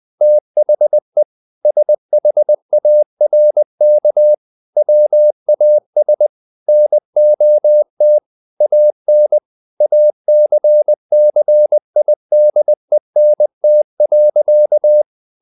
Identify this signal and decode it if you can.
Morse code, heard as a keyed tone, about 20 wpm, 600 Hz. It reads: THE SHARK WAS NOT AN ACCIDENT.
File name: the old man and the sea part 2 20wpm